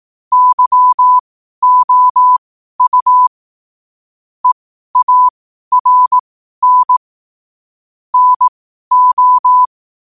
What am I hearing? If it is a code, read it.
Morse code, heard as a keyed tone, about 18 wpm, spaced as for 12 wpm, 1000 Hz. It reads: YOU EARN NO
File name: qotdfiles / 12wpm